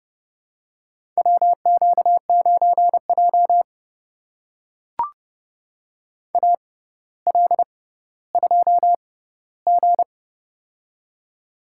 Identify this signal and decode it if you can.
Morse code, heard as a keyed tone, about 32 wpm, spaced as for 16 wpm, 700 Hz. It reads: WQ9J AL2G
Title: USA Call Signs - 4 Characters - 6x Character Spacing - 30wpm